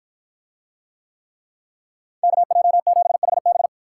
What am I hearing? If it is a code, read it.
Morse code, heard as a keyed tone, about 47 wpm, 700 Hz. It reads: KJ7HB